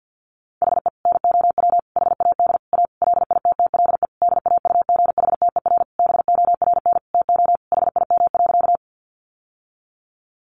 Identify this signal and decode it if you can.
Morse code, heard as a keyed tone, about 50 wpm, 700 Hz. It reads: HE NOW HAD A LITTLE DAUGHTER BORN TO HIM.
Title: Sentences from Top 1500 Words 50wpm